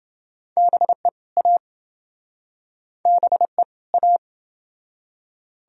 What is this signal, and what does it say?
Morse code, heard as a keyed tone, about 29 wpm, 700 Hz. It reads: BE A BE A